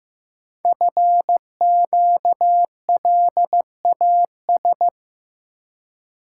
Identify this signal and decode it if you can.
Morse code, heard as a keyed tone, about 15 wpm, 700 Hz. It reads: FQLAS
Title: Code Groups - 5 Letters - 15wpm